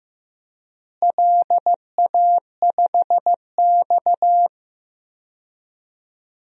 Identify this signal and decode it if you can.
Morse code, heard as a keyed tone, about 16 wpm, 700 Hz. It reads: LA5X
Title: DXCC-100 Call Signs - 4 Characters - Repeated 3x - 15wpm